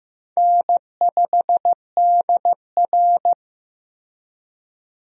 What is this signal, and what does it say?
Morse code, heard as a keyed tone, about 15 wpm, 700 Hz. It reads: N5DR
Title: USA Call Signs - 4 Characters - Repeated 2x - 15wpm